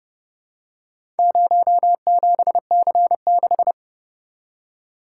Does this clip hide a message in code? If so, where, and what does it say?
Yes: Morse code from 1.2 to 3.7 s, heard as a keyed tone, about 29 wpm, 700 Hz. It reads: 07C6